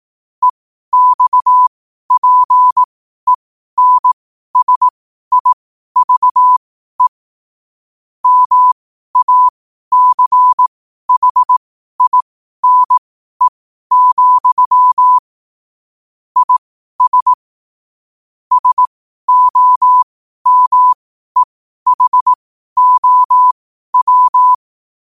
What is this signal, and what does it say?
Morse code, heard as a keyed tone, about 18 wpm, spaced as for 12 wpm, 1000 Hz. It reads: EXPENSIVE MACHINE, IS SOMEHOW
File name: qotdfiles / 12wpm